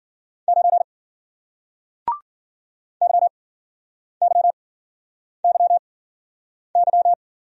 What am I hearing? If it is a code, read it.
Morse code, heard as a keyed tone, about 53 wpm, spaced as for 9 wpm, 700 Hz. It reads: 9 YYYY